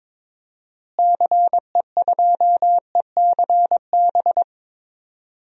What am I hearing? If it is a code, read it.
Morse code, heard as a keyed tone, about 22 wpm, 700 Hz. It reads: CE2ECB
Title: DXCC-100 Call Signs - 6 Characters - Repeated 3x - 22wpm